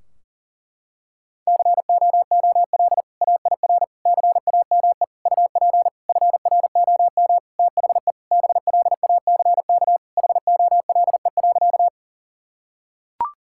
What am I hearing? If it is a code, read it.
Morse code, heard as a keyed tone, about 40 wpm, 700 Hz. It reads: COOL AIR CAME UP FROM THE BLACK HOLE.
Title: Sentences from Top 1600 Words 40wpm